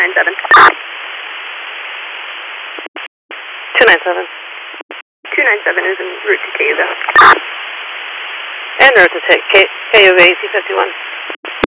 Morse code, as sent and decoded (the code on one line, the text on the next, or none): none